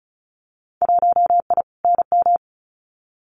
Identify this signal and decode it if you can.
Morse code, heard as a keyed tone, about 34 wpm, 700 Hz. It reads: 1I NM
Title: Winter Field Day Exchanges Encore 35wpm